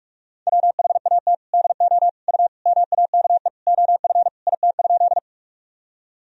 Morse code, heard as a keyed tone, about 46 wpm, 700 Hz. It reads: WHAT DO U MAKE OF IT?
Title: Sentences from Top 100 Words and Abbreviations 45wpm